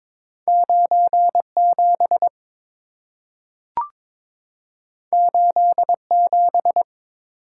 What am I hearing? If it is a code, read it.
Morse code, heard as a keyed tone, about 22 wpm, 700 Hz. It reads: 97 87
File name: practice / 22wpm / Two Letter-Number Combinations - 22wpm